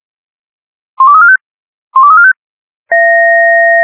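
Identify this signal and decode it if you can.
Five-tone selective call (ZVEI), heard as quick runs of five tones, tones of about 70 ms. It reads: 12345 12345
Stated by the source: ZVEI five-tone selective call 12345 received on an SDR.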